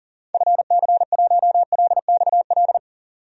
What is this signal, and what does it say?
Morse code, heard as a keyed tone, about 40 wpm, 700 Hz. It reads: FC1LXL
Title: DXCC-100 Call Signs - 6 Characters - 40wpm